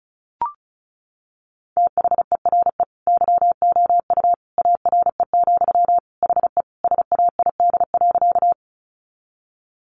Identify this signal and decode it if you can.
Morse code, heard as a keyed tone, about 34 wpm, 700 Hz. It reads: THERE YOU ARE, HE SAID.
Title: Sentences from Top 100 Words 35wpm